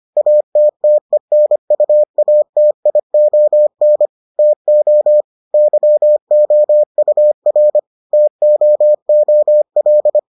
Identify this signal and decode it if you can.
Morse code, heard as a keyed tone, about 25 wpm, 600 Hz. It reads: ATTENUATION TO YOUR TOOL